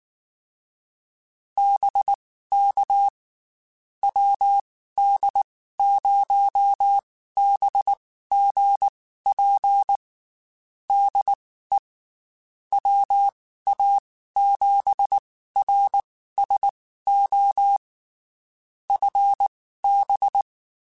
Morse code, heard as a keyed tone, about 19 wpm, spaced as for 14 wpm, 800 Hz. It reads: BK WD0BGP DE WA7RSO FB
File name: QSOs / QSOs-mp3s / WD0BGP 1@18